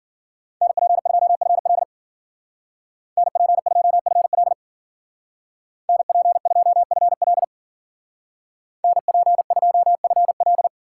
Morse code, heard as a keyed tone, about 53 wpm, 700 Hz. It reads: NP2FL NP2FL NP2FL NP2FL